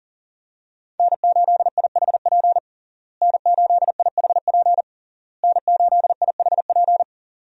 Morse code, heard as a keyed tone, about 40 wpm, 700 Hz. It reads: N8IHP N8IHP N8IHP
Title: USA Call Signs - 5 Characters - Repeated 3x - 40wpm